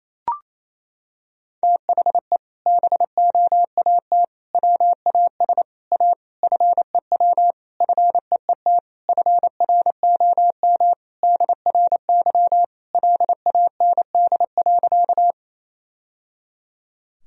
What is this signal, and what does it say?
Morse code, heard as a keyed tone, about 28 wpm, 700 Hz. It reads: THE BOAT WAS A FEW FEET FROM DRY LAND.
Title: Sentences from Top 1000 Words - Review Part 3 - 28wpm